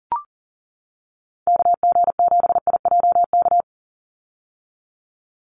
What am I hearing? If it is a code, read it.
Morse code, heard as a keyed tone, about 40 wpm, 690 Hz. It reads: KG7IJK
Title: USA Call Signs 40wpm